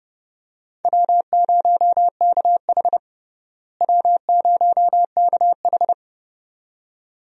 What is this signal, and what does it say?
Morse code, heard as a keyed tone, about 30 wpm, 700 Hz. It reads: W0KH W0KH
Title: USA Call Signs - 4 Characters - Repeated 2x - 30wpm